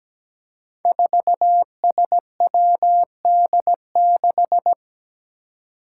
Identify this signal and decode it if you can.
Morse code, heard as a keyed tone, about 17 wpm, 700 Hz. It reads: 4SWD6
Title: Five Challenging Letter-Number Combinations 17wpm